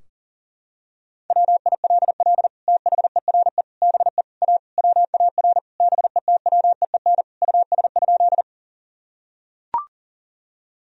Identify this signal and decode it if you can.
Morse code, heard as a keyed tone, about 37 wpm, 700 Hz. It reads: WILL THERE BE A WAR BETWEEN US?
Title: Sentences from Top 1000 Words - Review Part 2 - 40wpm